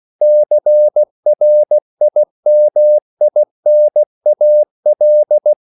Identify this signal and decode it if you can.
Morse code, heard as a keyed tone, about 16 wpm, 600 Hz. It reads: CRIMINAL